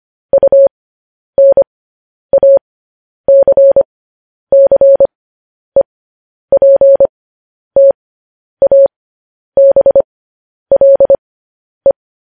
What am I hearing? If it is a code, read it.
Morse code, heard as a keyed tone, about 25 wpm, spaced as for 10 wpm, 550 Hz. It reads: UNACCEPTABLE
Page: news headlines in morse code at 10 wpm